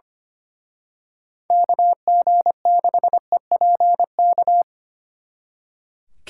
Morse code, heard as a keyed tone, about 28 wpm, 700 Hz. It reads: KG6EPK
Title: Speed Racing - USA 2x3 Call Signs 25wpm